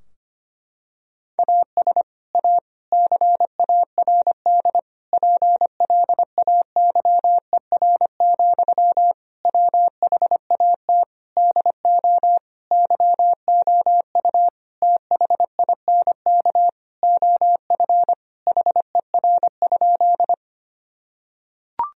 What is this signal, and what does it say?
Morse code, heard as a keyed tone, about 25 wpm, 700 Hz. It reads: AS A CARD PLAYER, WHAT DO YOU THINK OF HER?
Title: Sentences from Top 1000 Words - Review Part 4 - 25wpm